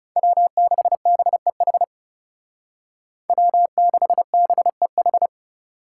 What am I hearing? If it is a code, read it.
Morse code, heard as a keyed tone, about 35 wpm, 700 Hz. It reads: W6BEH W6BEH
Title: Speed Racing - USA 1x3 Call Signs 30wpm